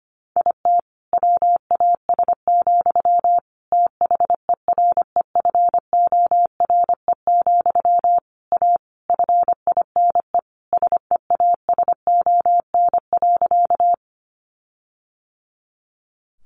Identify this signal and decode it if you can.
Morse code, heard as a keyed tone, about 25 wpm, 700 Hz. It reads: IT WAS, THEREFORE, A FINE SEASON.
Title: Sentences from Top 1400 Words 25wpm